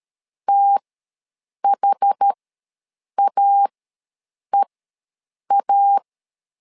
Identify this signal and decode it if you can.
Morse code, heard as a keyed tone, about 13 wpm, spaced as for 7 wpm, 800 Hz. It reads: THAEA
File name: LEZIONI / 3 MWRO / 3 M ws 12wpm 1745